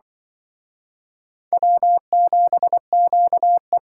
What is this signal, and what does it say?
Morse code, heard as a keyed tone, about 25 wpm, 700 Hz. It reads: W7QE
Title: Speed Racing - USA 1x2 Call Signs 20wpm